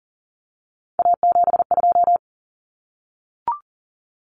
Morse code, heard as a keyed tone, about 40 wpm, 700 Hz. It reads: A72
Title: Common Cut Numbers - 3-Digit 40wpm